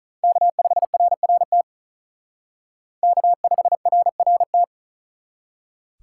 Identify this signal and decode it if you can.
Morse code, heard as a keyed tone, about 39 wpm, 700 Hz. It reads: K5RRT K5RRT
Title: Speed Racing - USA 1x3 Call Signs 35wpm